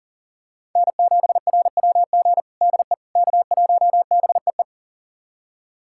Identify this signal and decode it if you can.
Morse code, heard as a keyed tone, about 40 wpm, 700 Hz. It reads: N7RWG DE K1BEE
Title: USA Callsign DE Callsign - 5 Characters - 40wpm